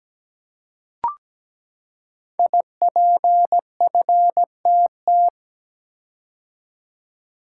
Morse code, heard as a keyed tone, about 17 wpm, 700 Hz. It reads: IPFTT
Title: Code Groups - 5 Letters - 17wpm